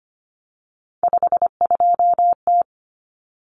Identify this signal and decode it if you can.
Morse code, heard as a keyed tone, about 25 wpm, 700 Hz. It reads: 52T
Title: Common Cut Numbers - 3-Digit 25wpm